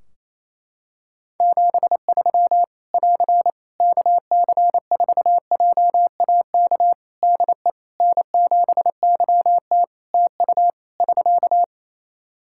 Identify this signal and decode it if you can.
Morse code, heard as a keyed tone, about 28 wpm, 700 Hz. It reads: 73 <AR> KC4JAK DE N7YT TU <SK>
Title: Practice QSO - Exchange 4 - Ending - 28wpm